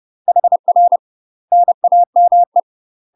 Morse code, heard as a keyed tone, about 30 wpm, 700 Hz. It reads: HR NAME
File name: UK MASTER QSOs TRAINING / 30WPM / UK MASTER QSOs 0033